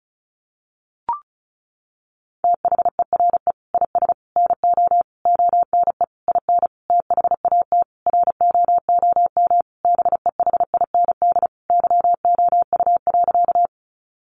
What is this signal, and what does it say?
Morse code, heard as a keyed tone, about 34 wpm, 700 Hz. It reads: THERE IS NO ONE IN THAT ROOM BEHIND YOU.
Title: Sentences from Top 500 Words 35wpm